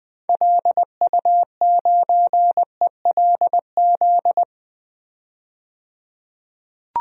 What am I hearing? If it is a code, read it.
Morse code, heard as a keyed tone, about 20 wpm, 700 Hz. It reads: LU9ELZ
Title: DXCC-100 Call Signs - 6 Characters - 20wpm